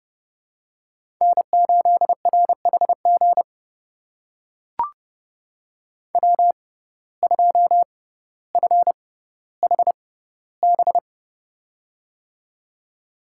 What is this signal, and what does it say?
Morse code, heard as a keyed tone, about 30 wpm, spaced as for 15 wpm, 700 Hz. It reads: N8RHG W2FHB
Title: USA Call Signs - 5 Characters - 6x Character Spacing - 30wpm